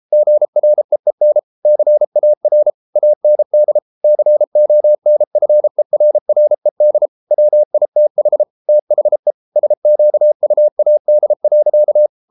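Morse code, heard as a keyed tone, about 33 wpm, 600 Hz. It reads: GREEN CAR AND CONFERRED WITH THE SQUAD.
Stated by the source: Morse code practice at 32 wpm from CW med Gnister, reading Ken Follett's novel Never in English.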